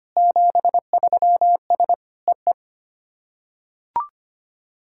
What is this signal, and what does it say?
Morse code, heard as a keyed tone, about 25 wpm, 700 Hz. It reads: 73S E E
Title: SOTA QSO Practice 25wpm